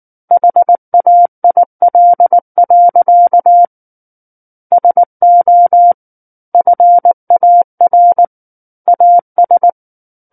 Morse code, heard as a keyed tone, about 19 wpm, 700 Hz. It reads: HAIL. SO FAR AS
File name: Book One / 10-Chapter 10 - 19 WPM